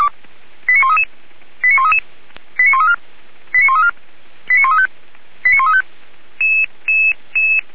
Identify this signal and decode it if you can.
Five-tone selective call (ZVEI), heard as quick runs of five tones, tones of about 70 ms. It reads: 89133 89133 89134 89134 89135 89135